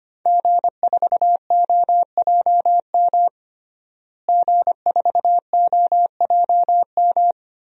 Morse code, heard as a keyed tone, about 25 wpm, 700 Hz. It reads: G4OJM G4OJM
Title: DXCC-100 Call Signs - 5 Characters - Repeated 3x - 25wpm